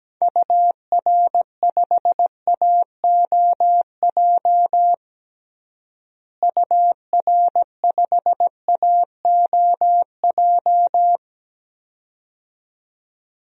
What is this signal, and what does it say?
Morse code, heard as a keyed tone, about 17 wpm, 700 Hz. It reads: UR5AOJ UR5AOJ
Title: DXCC-100 Call Signs - 6 Characters - Repeated 2x - 17wpm